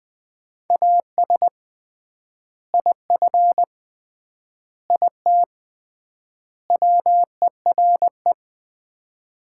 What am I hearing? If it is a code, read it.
Morse code, heard as a keyed tone, about 21 wpm, 700 Hz. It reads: AS IF IT WERE
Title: 3x Word-Spacing - Sets of 4 Words - 20wpm